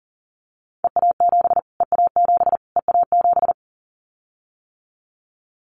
Morse code, heard as a keyed tone, about 40 wpm, 700 Hz. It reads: EA7 EA7 EA7
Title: Callsign Prefixes for Top 25 DXCC - Repeated 3x - 40wpm